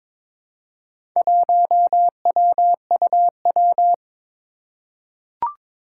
Morse code, heard as a keyed tone, about 23 wpm, 700 Hz. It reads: 1WUW
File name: practice / 22wpm / Four Letter-Number Combinations - 22wpm